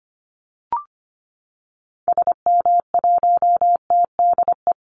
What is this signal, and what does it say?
Morse code, heard as a keyed tone, about 25 wpm, 700 Hz. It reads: SM1TDE
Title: Call Signs Repeated 2x - 25wpm